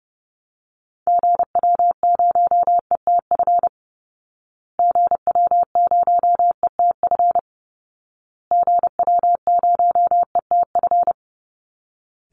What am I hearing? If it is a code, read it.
Morse code, heard as a keyed tone, about 28 wpm, 700 Hz. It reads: GW0ETF GW0ETF GW0ETF
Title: Call Signs Repeated 2x - 30wpm